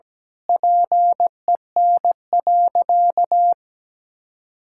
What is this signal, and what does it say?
Morse code, heard as a keyed tone, about 17 wpm, 700 Hz. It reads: PEN.
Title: Pangram Challenge v2 - 17wpm